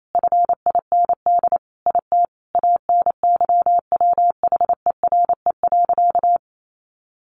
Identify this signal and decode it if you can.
Morse code, heard as a keyed tone, about 28 wpm, 700 Hz. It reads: FIND IT ANYWHERE.
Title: Sentences from Top 1400 Words 28wpm